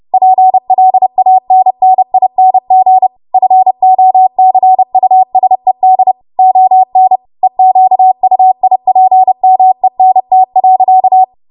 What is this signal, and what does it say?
Morse code, heard as a keyed tone, about 30 wpm, 750 Hz. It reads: PLANNING FOCUSED ON EQUIPMENT.
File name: Morse / 30WPM